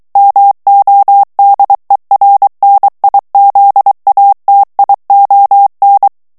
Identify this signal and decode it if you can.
Morse code, heard as a keyed tone, about 23 wpm, 800 Hz. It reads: MODERNIZATION
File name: ANGLAIS texte clair 3 vitesse 22 mots minute